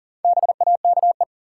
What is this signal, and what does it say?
Morse code, heard as a keyed tone, about 40 wpm, 700 Hz. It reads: BAKE
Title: Four Letter Words - 40wpm